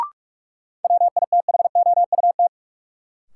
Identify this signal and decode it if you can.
Morse code, heard as a keyed tone, about 46 wpm, 700 Hz. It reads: WITHOUT
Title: Sentences from Top 600 Words - New Word Drill - 45wpm